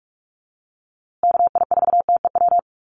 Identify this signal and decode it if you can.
Morse code, heard as a keyed tone, about 44 wpm, 700 Hz. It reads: KI4TEW
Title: USA Call Signs Repeated 3x 45wpm